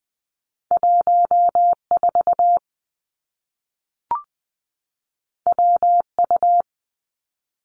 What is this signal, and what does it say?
Morse code, heard as a keyed tone, about 20 wpm, 700 Hz. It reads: 1 4 W U
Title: Two Challenging Letter-Number Combinations 20wpm